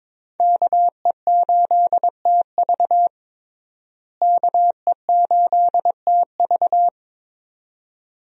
Morse code, heard as a keyed tone, about 21 wpm, 700 Hz. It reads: KE8TV KE8TV
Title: USA Call Signs - 5 Characters - Repeated 2x - 22wpm